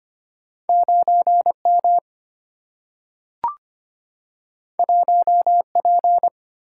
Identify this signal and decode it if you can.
Morse code, heard as a keyed tone, about 25 wpm, 700 Hz. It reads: 9 M 1 P
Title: Single-Letter Single-Number 25wpm